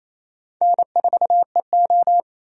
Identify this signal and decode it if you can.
Morse code, heard as a keyed tone, about 28 wpm, 700 Hz. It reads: N4EO
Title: USA Call Signs - 4 Characters - Repeated 2x - 28wpm